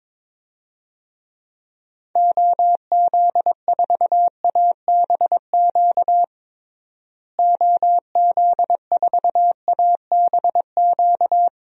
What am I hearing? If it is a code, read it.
Morse code, heard as a keyed tone, about 21 wpm, 700 Hz. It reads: OZ4ABQ OZ4ABQ
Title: DXCC-100 Call Signs - 6 Characters - Repeated 2x - 22wpm